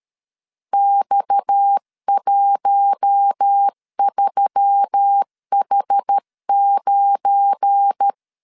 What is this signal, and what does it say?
Morse code, heard as a keyed tone, about 13 wpm, 800 Hz. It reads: X13H9
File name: LEZIONI / 8 YJ45 / 8 YJ45 12wpm 2151